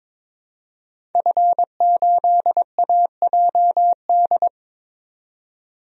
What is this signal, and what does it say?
Morse code, heard as a keyed tone, about 22 wpm, 700 Hz. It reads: F8AJD
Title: DXCC-100 Call Signs - 5 Characters - 22wpm